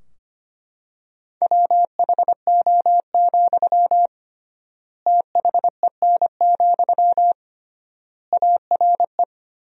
Morse code, heard as a keyed tone, about 25 wpm, 700 Hz. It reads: WHO, THEN, ARE
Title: 3x Word-Spacing - Sentences from Top 100 Words - 25wpm